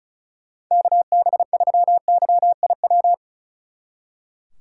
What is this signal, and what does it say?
Morse code, heard as a keyed tone, about 46 wpm, 700 Hz. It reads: KB3YIW